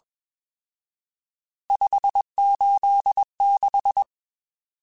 Morse code, heard as a keyed tone, about 20 wpm, 800 Hz. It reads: 586